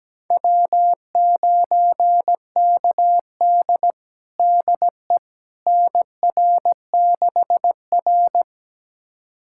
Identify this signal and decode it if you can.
Morse code, heard as a keyed tone, about 17 wpm, 700 Hz. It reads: W9KD DE NR6R
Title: USA Callsign DE Callsign - 4 Characters - 17wpm